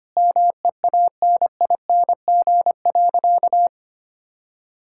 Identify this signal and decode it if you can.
Morse code, heard as a keyed tone, about 25 wpm, 700 Hz. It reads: MEANING.
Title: Sentences from Top 1000 Words - Review Part 6 - 25wpm